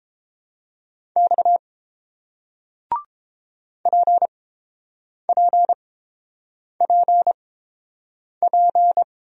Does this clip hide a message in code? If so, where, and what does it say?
Yes: Morse code from 1.2 to 9.0 s, heard as a keyed tone, about 30 wpm, spaced as for 7 wpm, 700 Hz. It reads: X PPPP